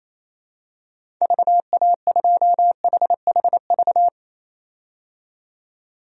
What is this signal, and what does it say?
Morse code, heard as a keyed tone, about 28 wpm, 700 Hz. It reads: VA2HHV
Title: DXCC-100 Call Signs - 6 Characters - 28wpm